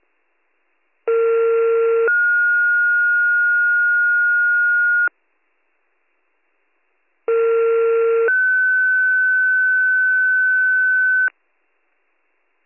Two-tone page, heard as a one-second tone, then a three-second tone, about 460 then 1430 Hz, about 460 then 1500 Hz. It is heard from 1.1 to 11.3 s.